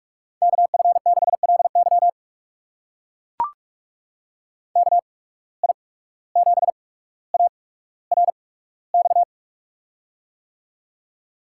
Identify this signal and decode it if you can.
Morse code, heard as a keyed tone, about 44 wpm, spaced as for 18 wpm, 700 Hz. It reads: KF6LY KI7ARX
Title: USA Call Signs - 8x Character Spacing - 45wpm